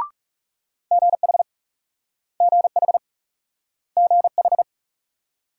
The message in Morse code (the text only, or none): GH GH GH